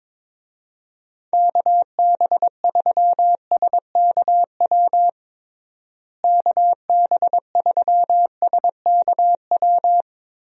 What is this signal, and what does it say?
Morse code, heard as a keyed tone, about 22 wpm, 700 Hz. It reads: KB3SKW KB3SKW
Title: USA Call Signs - 6 Characters - Repeated 3x - 22wpm